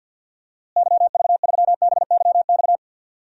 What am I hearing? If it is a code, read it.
Morse code, heard as a keyed tone, about 47 wpm, 700 Hz. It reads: YV3BYX
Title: DXCC-100 Call Signs - 6 Characters - Repeated 2x - 50wpm